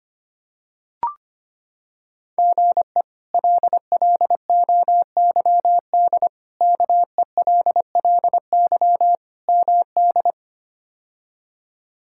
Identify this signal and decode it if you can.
Morse code, heard as a keyed tone, about 25 wpm, 700 Hz. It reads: GE LLOYD KELLY MD
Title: K1USN SST QSO Practice - 25wpm